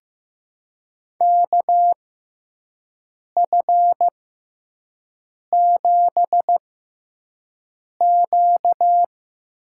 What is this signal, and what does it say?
Morse code, heard as a keyed tone, about 15 wpm, spaced as for 5 wpm, 700 Hz. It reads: KF7Q
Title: USA Call Signs - 6x Character Spacing - 15wpm